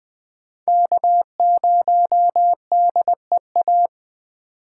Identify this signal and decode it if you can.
Morse code, heard as a keyed tone, about 21 wpm, 700 Hz. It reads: K0DEA
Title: USA Call Signs - 5 Characters - 3x Character Spacing - 20wpm